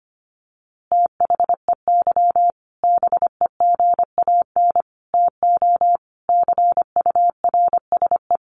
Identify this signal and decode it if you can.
Morse code, heard as a keyed tone, about 25 wpm, 700 Hz. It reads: THEY BEGAN TO CURSE
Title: A Sound Of Thunder 25wpm